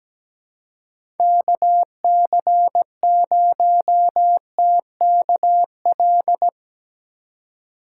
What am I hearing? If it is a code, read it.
Morse code, heard as a keyed tone, about 17 wpm, 700 Hz. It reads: KC0TKL
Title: USA Call Signs - 6 Characters - 3x Character Spacing - 17wpm